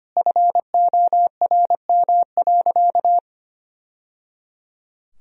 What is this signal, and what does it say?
Morse code, heard as a keyed tone, about 25 wpm, 700 Hz. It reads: FORM.
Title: Sentences from Top 1000 Words - Review Part 1 - 25wpm